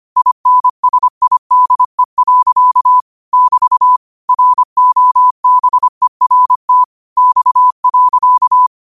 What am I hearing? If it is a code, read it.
Morse code, heard as a keyed tone, about 25 wpm, 1000 Hz. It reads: INSIDE. <BT> ROBERT X.
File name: qotdfiles / 25wpm